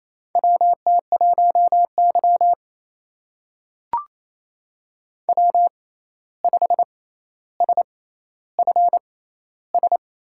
Morse code, heard as a keyed tone, about 28 wpm, spaced as for 13 wpm, 700 Hz. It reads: WT1Y W5SFS
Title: USA Call Signs - 6x Character Spacing - 28wpm